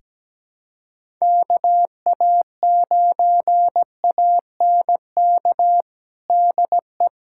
Morse code, heard as a keyed tone, about 17 wpm, 700 Hz. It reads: KA9ANK DE
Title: USA Callsign DE Callsign - 6 Characters - 17wpm